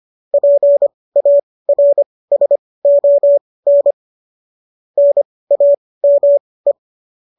Morse code, heard as a keyed tone, about 25 wpm, spaced as for 18 wpm, 570 Hz. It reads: PARSON NAME